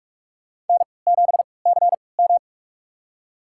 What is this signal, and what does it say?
Morse code, heard as a keyed tone, about 47 wpm, spaced as for 26 wpm, 700 Hz. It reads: N7CM
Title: USA Call Signs - 4 Characters - 3x Character Spacing - 45wpm